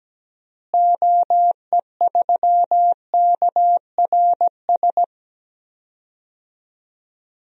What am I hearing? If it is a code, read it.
Morse code, heard as a keyed tone, about 17 wpm, 700 Hz. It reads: OE3KRS
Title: DXCC-100 Call Signs - 6 Characters - Repeated 2x - 17wpm